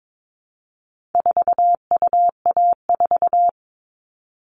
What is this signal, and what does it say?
Morse code, heard as a keyed tone, about 22 wpm, 700 Hz. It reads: 4UA4
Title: All Cut Numbers - 4-Digit 22wpm